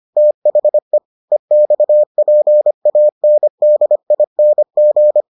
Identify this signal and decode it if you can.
Morse code, heard as a keyed tone, about 25 wpm, 600 Hz. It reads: THE EXPANDING